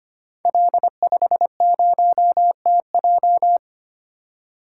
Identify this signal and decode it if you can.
Morse code, heard as a keyed tone, about 25 wpm, 700 Hz. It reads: L50TJ
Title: Code Groups - 5 Letter:Numbers - 25wpm